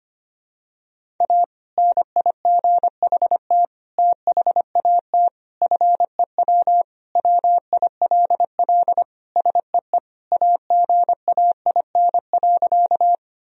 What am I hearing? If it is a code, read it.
Morse code, heard as a keyed tone, about 25 wpm, 700 Hz. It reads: A NIGHT THAT FEW WILL SEE AGAIN.
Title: Sentences from Top 1000 Words - Review Part 1 - 25wpm